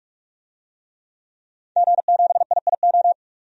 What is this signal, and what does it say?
Morse code, heard as a keyed tone, about 44 wpm, 690 Hz. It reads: G7IIO
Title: DXCC-100 Call Signs - 5 Characters - 45wpm